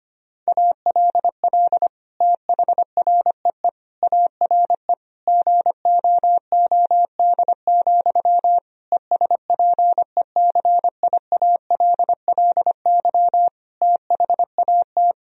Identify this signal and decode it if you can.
Morse code, heard as a keyed tone, about 25 wpm, 700 Hz. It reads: ALL THREE ARE GOOD, ESPECIALLY THAT
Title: Sentences from Top 1000 Words - Review Part 4 - 25wpm